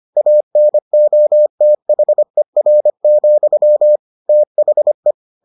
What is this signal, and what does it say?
Morse code, heard as a keyed tone, about 25 wpm, 600 Hz. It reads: ANOTHER, THE